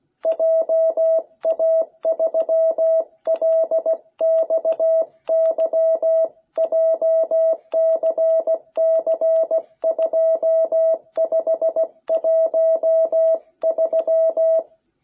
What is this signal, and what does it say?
Morse code, heard as a keyed tone, about 16 wpm, 660 Hz. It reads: JA3LXY JCC2513